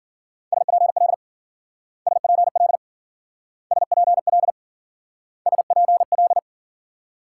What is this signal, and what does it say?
Morse code, heard as a keyed tone, about 50 wpm, 700 Hz. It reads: SPL SPL SPL SPL
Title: Speed-Racing Multi-letter Phonemes - 40wpm